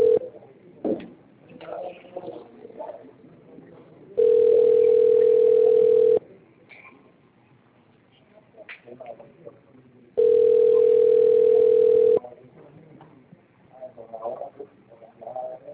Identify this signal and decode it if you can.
Ringback tone (North America), 440 + 480 Hz.